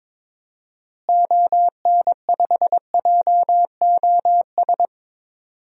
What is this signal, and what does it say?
Morse code, heard as a keyed tone, about 23 wpm, 700 Hz. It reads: ON5JOS
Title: DXCC-100 Call Signs - 6 Characters - Repeated 2x - 22wpm